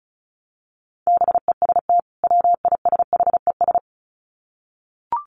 Morse code, heard as a keyed tone, about 34 wpm, 700 Hz. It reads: BEST WISHES
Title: Common QSO Phrases 35wpm